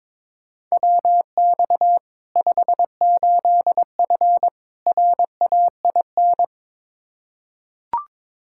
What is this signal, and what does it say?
Morse code, heard as a keyed tone, about 21 wpm, 700 Hz. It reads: WX 58F RAIN
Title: Practice QSO - Exchange 2 - Info - 22wpm